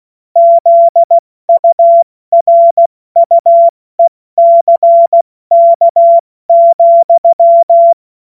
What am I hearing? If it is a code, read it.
Morse code, heard as a keyed tone, about 16 wpm, spaced as for 14 wpm, 680 Hz. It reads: ZURUECK,